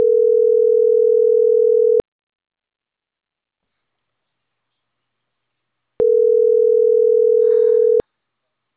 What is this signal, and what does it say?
Ringback tone (North America), 440 + 480 Hz.